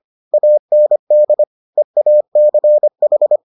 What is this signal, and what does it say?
Morse code, heard as a keyed tone, about 25 wpm, 600 Hz. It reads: AND EACH